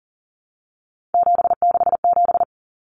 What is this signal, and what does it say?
Morse code, heard as a keyed tone, about 40 wpm, 700 Hz. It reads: 767
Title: Numbers - 3-Digit 40wpm